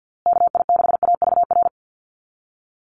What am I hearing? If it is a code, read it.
Morse code, heard as a keyed tone, about 50 wpm, 700 Hz. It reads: KI6AVR
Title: USA Call Signs Repeated 2x 50wpm